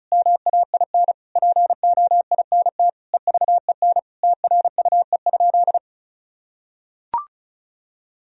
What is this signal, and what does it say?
Morse code, heard as a keyed tone, about 34 wpm, 700 Hz. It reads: MAIN POINT EVEN TRUE?
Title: Sentences from Top 1000 Words - Review Part 2 - 35wpm